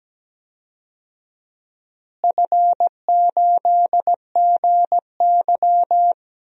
Morse code, heard as a keyed tone, about 17 wpm, 700 Hz. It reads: F8GY
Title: DXCC-100 Call Signs - 4 Characters - Repeated 2x - 17wpm